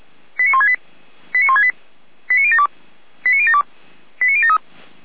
Five-tone selective call (ZVEI), heard as quick runs of five tones, tones of about 80 ms. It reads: 89168 89168 89072 89072 89073